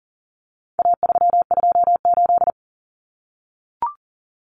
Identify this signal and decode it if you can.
Morse code, heard as a keyed tone, about 40 wpm, 700 Hz. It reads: A328
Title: Common Cut Numbers - 4-Digit 40wpm